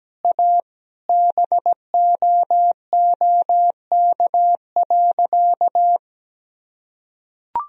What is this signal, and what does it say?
Morse code, heard as a keyed tone, about 17 wpm, 700 Hz. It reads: A BOOK.
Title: Sentences from Top 1000 Words - Review Part 1 - 17wpm